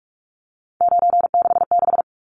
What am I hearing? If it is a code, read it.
Morse code, heard as a keyed tone, about 44 wpm, 700 Hz. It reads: 966
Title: Numbers - 3-Digit 45wpm